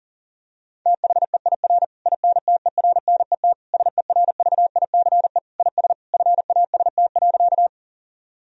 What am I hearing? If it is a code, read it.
Morse code, heard as a keyed tone, about 40 wpm, 700 Hz. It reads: THEIR INTERNET SERVICE IS FAST.
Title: Sentences from Top 1000 Words - Review Part 4 - 40wpm